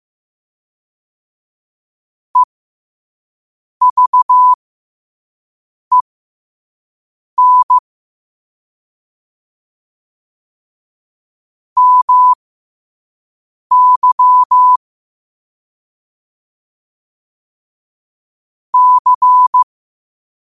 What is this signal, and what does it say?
Morse code, heard as a keyed tone, about 15 wpm, spaced as for 5 wpm, 1000 Hz. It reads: EVEN MY C
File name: qotdfiles / 5wpm